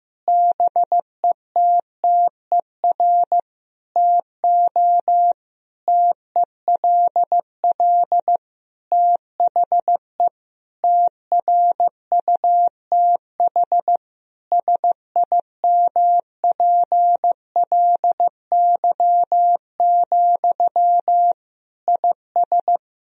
Morse code, heard as a keyed tone, about 15 wpm, 700 Hz. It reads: BETTER TO TELL THE TRUTH SIMPLY, IS
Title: Sentences from Top 1000 Words - Review Part 4 - 15wpm